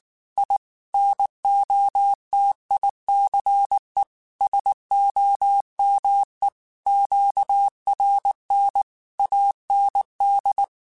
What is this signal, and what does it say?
Morse code, heard as a keyed tone, about 19 wpm, 800 Hz. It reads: I NOTICE SOME QRN AND
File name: QSOs / QSOs-mp3s / WD0BGP 1@18